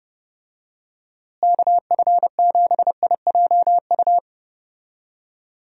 Morse code, heard as a keyed tone, about 32 wpm, 700 Hz. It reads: KF7IJU
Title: USA Call Signs - 6 Characters - Repeated 3x - 30wpm